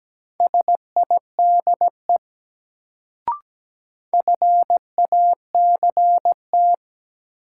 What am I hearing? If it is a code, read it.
Morse code, heard as a keyed tone, about 17 wpm, 700 Hz. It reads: SIDE FACT
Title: Sets of 2 Words - New Word Drill - 17wpm